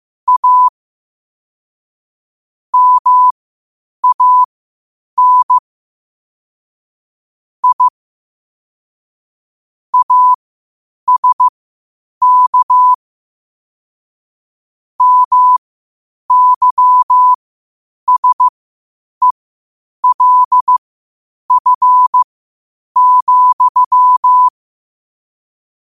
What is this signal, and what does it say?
Morse code, heard as a keyed tone, about 15 wpm, spaced as for 8 wpm, 1000 Hz. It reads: A MAN I ASK MYSELF,
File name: qotdfiles / 8wpm